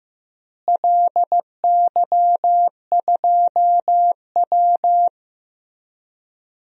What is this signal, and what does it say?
Morse code, heard as a keyed tone, about 15 wpm, 700 Hz. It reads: LY2W
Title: DXCC-100 Call Signs - 4 Characters - Repeated 2x - 15wpm